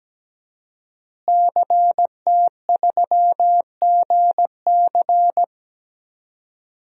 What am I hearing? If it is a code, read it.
Morse code, heard as a keyed tone, about 17 wpm, 700 Hz. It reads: CT3GC
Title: DXCC-100 Call Signs - 5 Characters - Repeated 3x - 17wpm